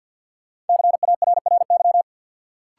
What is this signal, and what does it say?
Morse code, heard as a keyed tone, about 53 wpm, 700 Hz. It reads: CARRY
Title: Sentences from Top 400 Words - New Word Drill - 50wpm